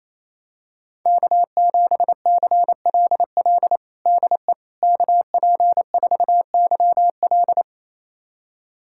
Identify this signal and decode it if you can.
Morse code, heard as a keyed tone, about 28 wpm, 700 Hz. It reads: K7CLL DE KP4YL
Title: USA Callsign DE Callsign - 5 Characters - 28wpm